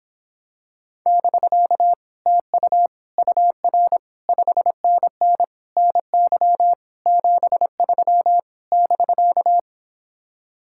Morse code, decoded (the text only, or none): <BK> TU UR 5NN NY 73 <BK>